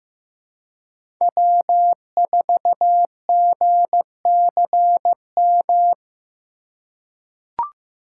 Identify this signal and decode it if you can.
Morse code, heard as a keyed tone, about 15 wpm, 700 Hz. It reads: W4GCM
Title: USA Call Signs - 5 Characters - 4x Character Spacing - 15wpm